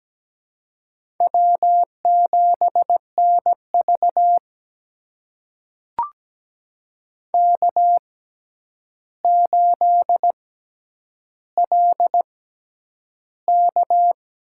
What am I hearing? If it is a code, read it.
Morse code, heard as a keyed tone, about 17 wpm, spaced as for 9 wpm, 700 Hz. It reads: W7NV K8LK
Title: USA Call Signs - 6x Character Spacing - 17wpm